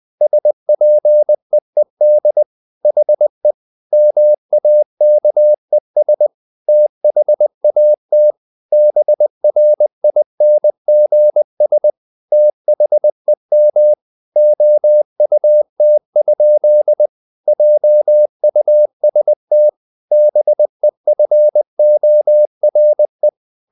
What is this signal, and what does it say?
Morse code, heard as a keyed tone, about 20 wpm, 600 Hz. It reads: SPEED HE MAKES THAT BRINGS THEM OUT? JUST BEFORE
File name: the old man and the sea part 2 20wpm